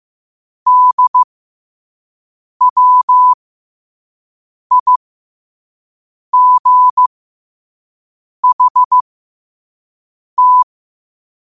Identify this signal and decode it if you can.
Morse code, heard as a keyed tone, about 15 wpm, spaced as for 5 wpm, 1000 Hz. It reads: DWIGHT
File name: qotdfiles / 5wpm